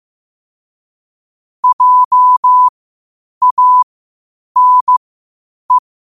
Morse code, heard as a keyed tone, about 15 wpm, spaced as for 8 wpm, 1000 Hz. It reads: JANE
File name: qotdfiles / 8wpm